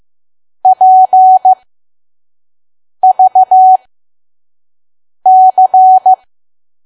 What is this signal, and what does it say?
Morse code, heard as a keyed tone, about 15 wpm, spaced as for 5 wpm, 750 Hz. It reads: PVC
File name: Archive / 5 WPM files / PM